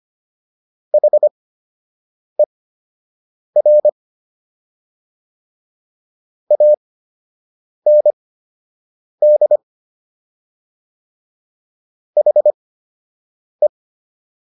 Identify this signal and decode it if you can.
Morse code, heard as a keyed tone, about 25 wpm, spaced as for 7 wpm, 600 Hz. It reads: HER AND HE